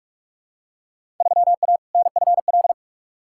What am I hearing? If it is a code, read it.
Morse code, heard as a keyed tone, about 46 wpm, 700 Hz. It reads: 3A NFL
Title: Field Day Exchanges - 45wpm